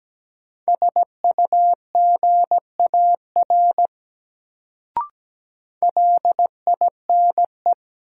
Morse code, heard as a keyed tone, about 17 wpm, 700 Hz. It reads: SUGAR LINE
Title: Sentences from Top 1200 Words - New Word Drill Alt - 17wpm